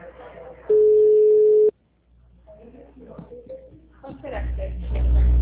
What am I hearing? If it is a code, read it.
Ringback tone (Europe, 425 Hz), with speech.